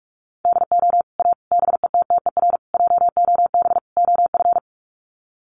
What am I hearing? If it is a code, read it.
Morse code, heard as a keyed tone, about 44 wpm, 700 Hz. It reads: DO A BETTER JOB OF
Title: Sets of 5 Words Encore2 45wpm